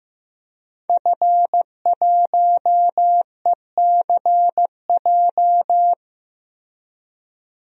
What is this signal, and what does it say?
Morse code, heard as a keyed tone, about 15 wpm, 700 Hz. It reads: F1ECJ
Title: DXCC-100 Call Signs - 5 Characters - 15wpm